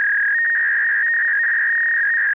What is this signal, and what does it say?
Fax machine handshake.